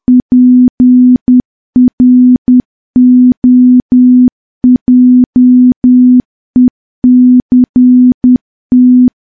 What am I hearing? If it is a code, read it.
Morse code, heard as a keyed tone, about 10 wpm, 250 Hz. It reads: PROJECT
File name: morse / morse-training-10WPM